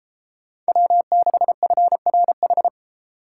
Morse code, heard as a keyed tone, about 33 wpm, 700 Hz. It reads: W6FRH